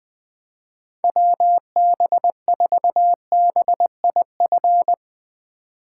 Morse code, heard as a keyed tone, about 20 wpm, 700 Hz. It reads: WB4BIF
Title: USA Call Signs - 6 Characters - 2x Character Spacing - 20wpm